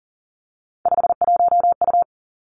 Morse code, heard as a keyed tone, about 40 wpm, 690 Hz. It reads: 51U
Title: Three Challenging Letter-Number Combinations 40wpm